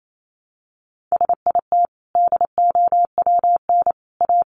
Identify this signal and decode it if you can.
Morse code, heard as a keyed tone, about 28 wpm, 700 Hz. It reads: SIT DOWN A
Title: Sentences from Top 1000 Words 28wpm